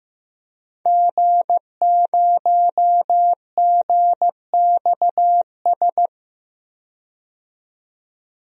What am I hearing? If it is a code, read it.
Morse code, heard as a keyed tone, about 15 wpm, 700 Hz. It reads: G0GXS
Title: DXCC-100 Call Signs - 5 Characters - 15wpm